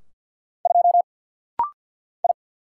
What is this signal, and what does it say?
Morse code, heard as a keyed tone, about 50 wpm, 700 Hz. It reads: 2 I
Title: Single Letter-Number - Rapid-Fire 50wpm